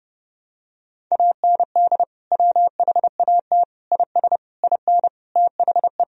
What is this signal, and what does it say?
Morse code, heard as a keyed tone, about 30 wpm, 700 Hz. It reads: AND WHAT IS IN THE
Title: Sentences from Top 1000 Words - Review Part 1 - 30wpm